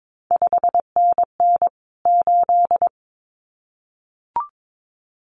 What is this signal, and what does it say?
Morse code, heard as a keyed tone, about 22 wpm, 700 Hz. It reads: 5NN 8
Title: CQ WW DX Contest Exchange - 22wpm